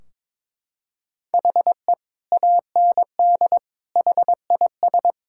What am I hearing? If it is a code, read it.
Morse code, heard as a keyed tone, about 22 wpm, 700 Hz. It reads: HE AND HIS
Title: Sentences from Top 1600 Words 22wpm